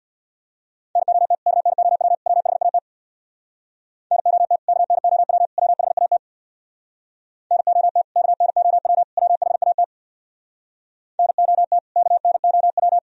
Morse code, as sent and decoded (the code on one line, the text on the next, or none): -. --- - / -.- -. --- .-- / .-- .... .- - / -. --- - / -.- -. --- .-- / .-- .... .- - / -. --- - / -.- -. --- .-- / .-- .... .- - / -. --- - / -.- -. --- .--
NOT KNOW WHAT NOT KNOW WHAT NOT KNOW WHAT NOT KNOW